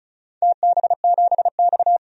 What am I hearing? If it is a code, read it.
Morse code, heard as a keyed tone, about 34 wpm, 700 Hz. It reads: TB7X
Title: DXCC-100 Call Signs - 4 Characters - 35wpm